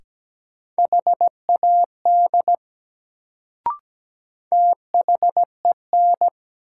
Morse code, heard as a keyed tone, about 17 wpm, 700 Hz. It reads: HAD THEN
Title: Sentences from Top 100 Words - New Word Drill - 17wpm